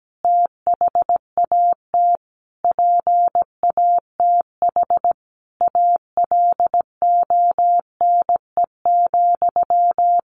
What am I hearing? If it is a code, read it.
Morse code, heard as a keyed tone, about 17 wpm, 700 Hz. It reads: THAT PATH ALONE,
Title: A Sound Of Thunder 17wpm